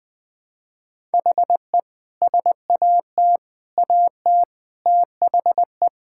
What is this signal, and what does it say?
Morse code, heard as a keyed tone, about 20 wpm, 700 Hz. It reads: HE SAT AT THE
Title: Sentences with 5-Word Phrases - 20wpm